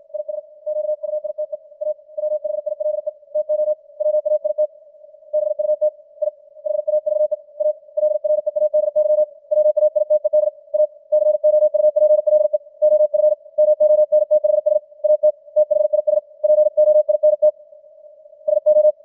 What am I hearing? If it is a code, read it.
Morse code, heard as a keyed tone, about 56 wpm, 610 Hz. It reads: IS QUITE A WHILE TO WAIT BUT I HAVE ALREADY WAITED A COUPLE OF MONTHS AT THIS POINT SO
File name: archived / CWBros